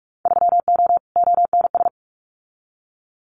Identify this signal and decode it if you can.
Morse code, heard as a keyed tone, about 46 wpm, 700 Hz. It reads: 3O ONS
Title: Winter Field Day Exchanges 45wpm